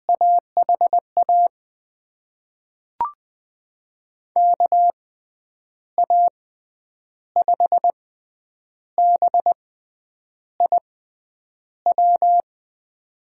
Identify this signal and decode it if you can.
Morse code, heard as a keyed tone, about 20 wpm, spaced as for 8 wpm, 700 Hz. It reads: AHA KA5BIW
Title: USA Call Signs - 6 Characters - 6x Character Spacing - 20wpm